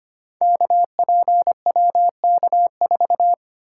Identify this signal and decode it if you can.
Morse code, heard as a keyed tone, about 25 wpm, 700 Hz. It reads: KPWK4
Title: Code Groups - 5 Letter:Numbers - 25wpm